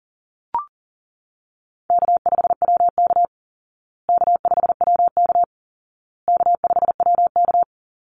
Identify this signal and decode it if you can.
Morse code, heard as a keyed tone, about 40 wpm, 700 Hz. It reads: K5WK K5WK K5WK
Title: Easy Call Signs Repeated 3x - 40wpm